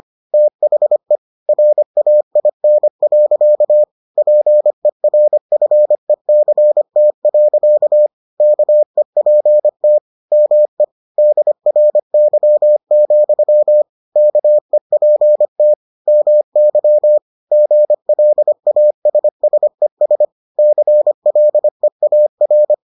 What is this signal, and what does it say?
Morse code, heard as a keyed tone, about 25 wpm, 600 Hz. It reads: THE RAIN. PERFECT. KEPT ME DRY, KEPT MY GLASSES CLEAR